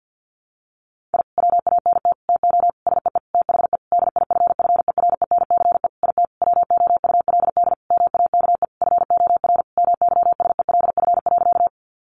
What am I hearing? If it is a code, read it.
Morse code, heard as a keyed tone, about 50 wpm, 700 Hz. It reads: I WANT TO SEE THE DIFFERENCE IT WOULD MAKE FOR MYSELF.
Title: Sentences from Top 1300 Words 50wpm